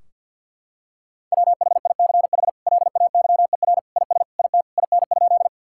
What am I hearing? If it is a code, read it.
Morse code, heard as a keyed tone, about 50 wpm, 700 Hz. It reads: WHICH LAYER IS IT IN?
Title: Sentences from Top 1600 Words 50wpm